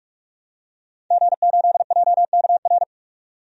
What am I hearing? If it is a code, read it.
Morse code, heard as a keyed tone, about 46 wpm, 700 Hz. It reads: G8JKR
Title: DXCC-100 Call Signs - 5 Characters - 45wpm